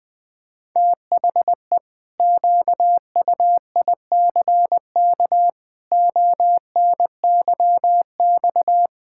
Morse code, heard as a keyed tone, about 21 wpm, 700 Hz. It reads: THE QUICK ONYX
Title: Pangram Challenge v2 - 20wpm